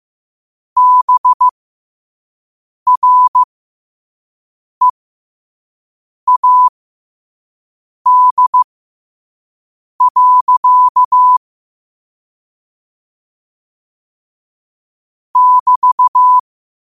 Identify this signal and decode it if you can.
Morse code, heard as a keyed tone, about 15 wpm, spaced as for 5 wpm, 1000 Hz. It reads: BREAD. <BT>
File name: qotdfiles / 5wpm